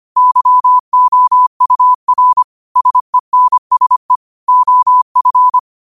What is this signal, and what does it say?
Morse code, heard as a keyed tone, about 25 wpm, 1000 Hz. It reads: YOUR SENSE OF
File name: qotdfiles / 25wpm